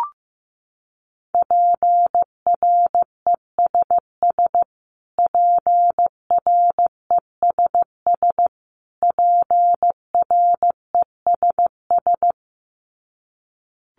Morse code, heard as a keyed tone, about 15 wpm, 700 Hz. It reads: PRESS PRESS PRESS
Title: Top 500 Words Repeated 15wpm